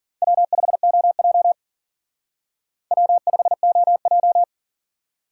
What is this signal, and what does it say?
Morse code, heard as a keyed tone, about 49 wpm, 700 Hz. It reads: W5OJ W5OJ
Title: Speed Racing - USA 1x2 Call Signs 40wpm